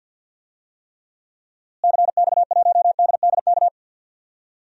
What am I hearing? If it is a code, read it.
Morse code, heard as a keyed tone, about 52 wpm, 700 Hz. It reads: CX1DDK
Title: DXCC-100 Call Signs - 6 Characters - Repeated 3x - 50wpm